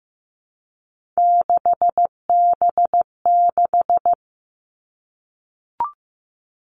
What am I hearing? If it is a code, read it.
Morse code, heard as a keyed tone, about 15 wpm, 700 Hz. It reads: 6B6
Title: All Cut Numbers - 3-Digit 15wpm